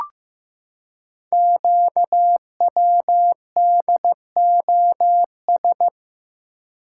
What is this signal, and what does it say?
Morse code, heard as a keyed tone, about 15 wpm, 700 Hz. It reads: QWDOS
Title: Code Groups - 5 Letters - 15wpm